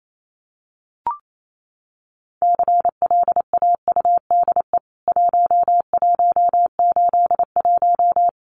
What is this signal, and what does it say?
Morse code, heard as a keyed tone, about 28 wpm, 700 Hz. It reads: CLAUDE 1181
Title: CWT Exchange 28wpm